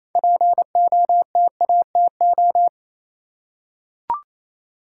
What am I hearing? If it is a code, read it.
Morse code, heard as a keyed tone, about 28 wpm, 700 Hz. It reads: POTATO
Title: Six Letter Words - 28wpm